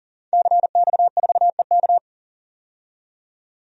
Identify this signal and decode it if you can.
Morse code, heard as a keyed tone, about 40 wpm, 700 Hz. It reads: CX4EK
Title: DXCC-100 Call Signs - 5 Characters - 40wpm